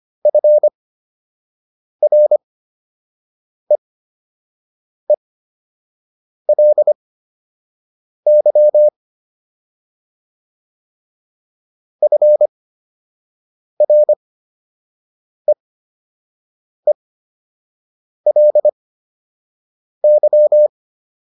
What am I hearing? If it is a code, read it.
Morse code, heard as a keyed tone, about 25 wpm, spaced as for 6 wpm, 600 Hz. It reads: FREELY FREELY